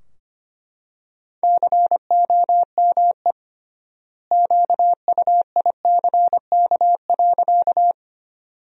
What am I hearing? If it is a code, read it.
Morse code, heard as a keyed tone, about 25 wpm, 700 Hz. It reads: COME QUICK.
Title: 3x Word-Spacing - Sentences from Top 500 Words - 25wpm